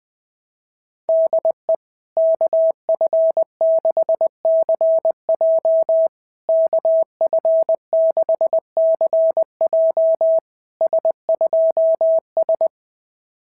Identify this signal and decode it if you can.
Morse code, heard as a keyed tone, about 21 wpm, 650 Hz. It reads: DE KF6CJ KF6CJ S2S
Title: SOTA QSO Practice 20wpm